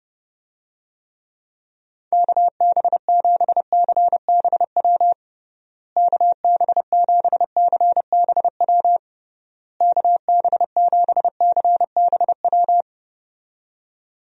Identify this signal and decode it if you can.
Morse code, heard as a keyed tone, about 29 wpm, 700 Hz. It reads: KB7CBW KB7CBW KB7CBW
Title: USA Call Signs - 6 Characters - Repeated 3x - 30wpm